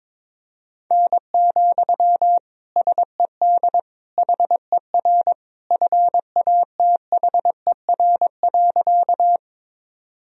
Morse code, heard as a keyed tone, about 22 wpm, 700 Hz. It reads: N, SED HER FATHER.
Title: Sentences from Top 300 Words and Abbreviations 22wpm